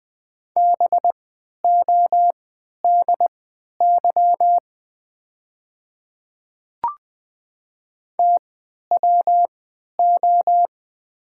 Morse code, heard as a keyed tone, about 20 wpm, spaced as for 11 wpm, 700 Hz. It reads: BODY TWO
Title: 3x Character Spacing - Top 500 Words - 20wpm